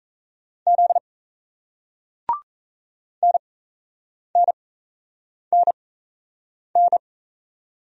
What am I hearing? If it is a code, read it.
Morse code, heard as a keyed tone, about 35 wpm, spaced as for 8 wpm, 700 Hz. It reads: Z NNNN